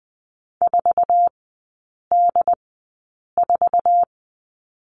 Morse code, heard as a keyed tone, about 20 wpm, spaced as for 8 wpm, 700 Hz. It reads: 4D4
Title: 107 Lesson 18e - Letter m - Three Characters 20wpm